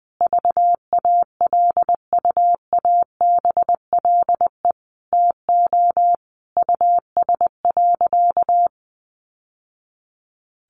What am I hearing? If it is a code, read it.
Morse code, heard as a keyed tone, about 20 wpm, 700 Hz. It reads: VALUABLE TO US.
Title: Sentences from Top 1100 Words 20wpm